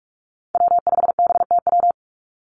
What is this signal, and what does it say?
Morse code, heard as a keyed tone, about 46 wpm, 700 Hz. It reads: W5BTW
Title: USA Call Signs 45wpm